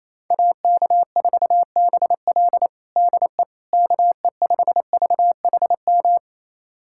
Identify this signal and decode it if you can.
Morse code, heard as a keyed tone, about 28 wpm, 700 Hz. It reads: AK4BL DE KE5VHM
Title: USA Callsign DE Callsign 28wpm